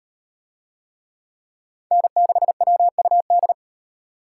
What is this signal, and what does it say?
Morse code, heard as a keyed tone, about 38 wpm, 700 Hz. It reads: N6WUD